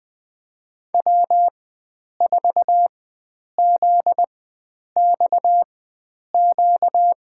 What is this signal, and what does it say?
Morse code, heard as a keyed tone, about 20 wpm, spaced as for 9 wpm, 700 Hz. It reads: W4ZXQ
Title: USA Call Signs - 5 Characters - 4x Character Spacing - 20wpm